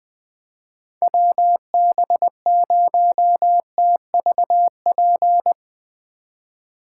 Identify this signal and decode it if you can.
Morse code, heard as a keyed tone, about 23 wpm, 700 Hz. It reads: WB0TVP
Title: Speed Racing - USA 2x3 Call Signs 20wpm